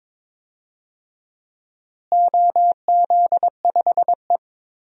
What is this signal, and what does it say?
Morse code, heard as a keyed tone, about 23 wpm, 700 Hz. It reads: OZ5E
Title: DXCC-100 Call Signs - 4 Characters - Repeated 3x - 22wpm